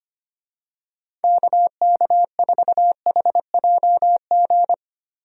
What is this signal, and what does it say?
Morse code, heard as a keyed tone, about 25 wpm, 700 Hz. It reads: KK4HJG
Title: USA Call Signs - 6 Characters - Repeated 3x - 25wpm